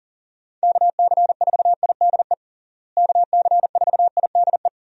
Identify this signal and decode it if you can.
Morse code, heard as a keyed tone, about 40 wpm, 700 Hz. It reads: KC4IDE KC4IDE
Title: USA Call Signs - 6 Characters - Repeated 2x - 40wpm